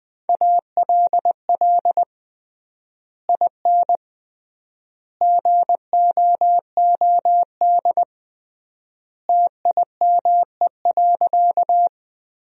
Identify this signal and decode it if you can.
Morse code, heard as a keyed tone, about 21 wpm, 700 Hz. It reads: ALL IN GOOD TIME.
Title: 3x Word-Spacing - Sentences from Top 200 Words - 20wpm